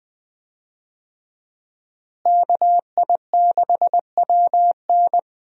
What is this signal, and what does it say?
Morse code, heard as a keyed tone, about 20 wpm, 700 Hz. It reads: KI6WN
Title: USA Call Signs - 5 Characters - Repeated 2x - 20wpm